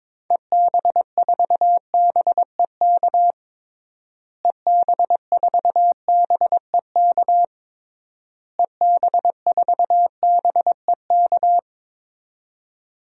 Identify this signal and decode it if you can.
Morse code, heard as a keyed tone, about 22 wpm, 700 Hz. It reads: EB4BEK EB4BEK EB4BEK
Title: DXCC-100 Call Signs - 6 Characters - Repeated 3x - 22wpm